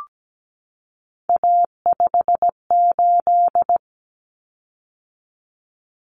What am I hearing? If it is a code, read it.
Morse code, heard as a keyed tone, about 17 wpm, 700 Hz. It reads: A58
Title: Common Cut Numbers - 3-Digit 17wpm